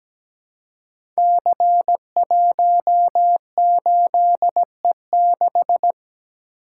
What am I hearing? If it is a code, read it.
Morse code, heard as a keyed tone, about 17 wpm, 700 Hz. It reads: C18E6
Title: Code Groups - 5 Letter:Numbers - 17wpm